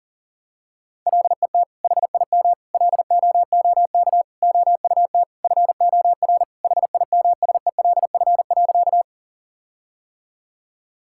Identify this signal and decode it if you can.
Morse code, heard as a keyed tone, about 40 wpm, 700 Hz. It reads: LET HIM LOOK OUT FOR HIMSELF.
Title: Sentences from Top 1600 Words 40wpm